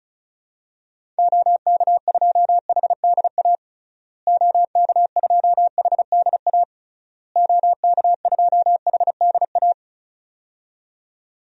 Morse code, heard as a keyed tone, about 36 wpm, 700 Hz. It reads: OK2HDA OK2HDA OK2HDA
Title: DXCC-100 Call Signs - 6 Characters - Repeated 3x - 35wpm